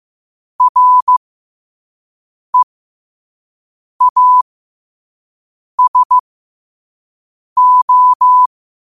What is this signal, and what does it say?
Morse code, heard as a keyed tone, about 15 wpm, spaced as for 5 wpm, 1000 Hz. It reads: REASO
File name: qotdfiles / 5wpm